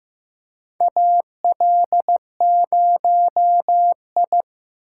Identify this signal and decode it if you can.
Morse code, heard as a keyed tone, about 15 wpm, 700 Hz. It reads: AL0I
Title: USA Call Signs - 4 Characters - 2x Character Spacing - 15wpm